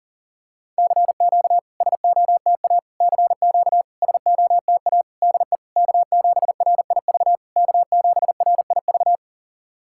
Morse code, heard as a keyed tone, about 44 wpm, 700 Hz. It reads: CQ SOTA CQ SOTA DE K7RIV K7RIV
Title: SOTA QSO Practice 40wpm